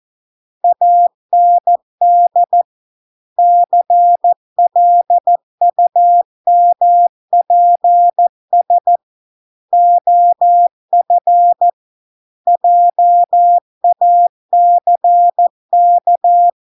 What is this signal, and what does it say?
Morse code, heard as a keyed tone, about 14 wpm, 700 Hz. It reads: AND CLUMPS OF JACK